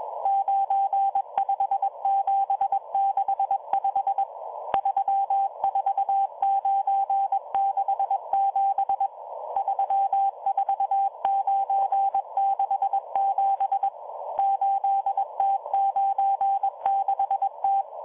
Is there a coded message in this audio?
Yes: Morse code from 0.3 to 17.8 s, heard as a keyed tone, about 22 wpm, 790 Hz, with radio noise. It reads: 95765 34967 34967 8T96T